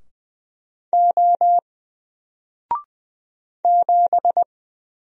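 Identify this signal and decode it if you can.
Morse code, heard as a keyed tone, about 20 wpm, 700 Hz. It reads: O 7
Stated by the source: Single letter and number Morse drill at 20 wpm, from N4LCD.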